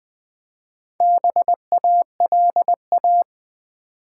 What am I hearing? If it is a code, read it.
Morse code, heard as a keyed tone, about 20 wpm, 700 Hz. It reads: BALA
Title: Build A Word - 2 Letters At A Time - 8-Letter Words - 20wpm